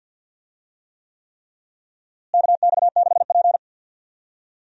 Morse code, heard as a keyed tone, about 50 wpm, 700 Hz. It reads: KX6P
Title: USA Call Signs - 4 Characters - Repeated 3x - 50wpm